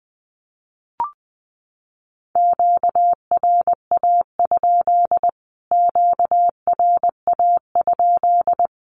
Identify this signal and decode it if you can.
Morse code, heard as a keyed tone, about 20 wpm, 700 Hz. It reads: QRA? QRA?
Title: Q-Codes Repeated 3x 20wpm